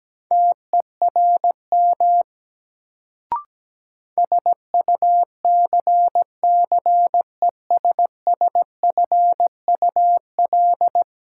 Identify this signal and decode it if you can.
Morse code, heard as a keyed tone, about 17 wpm, 700 Hz. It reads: TERM SUCCESSFUL
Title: Sentences from Top 900 Words - New Word Drill - 17wpm